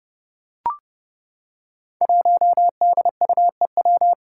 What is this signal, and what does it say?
Morse code, heard as a keyed tone, about 32 wpm, 700 Hz. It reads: 1DUEW
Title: Five Challenging Letter-Number Combinations 30wpm